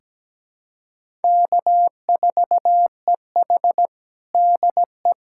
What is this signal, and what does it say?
Morse code, heard as a keyed tone, about 17 wpm, 700 Hz. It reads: K4EH DE
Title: USA Callsign DE Callsign - 4 Characters - 17wpm